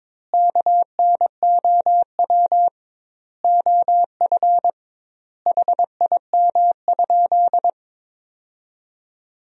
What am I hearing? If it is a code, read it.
Morse code, heard as a keyed tone, about 22 wpm, 700 Hz. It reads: KNOW OF HIM?
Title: 2x Word-Spacing - Sentences from Top 100 Words - 22wpm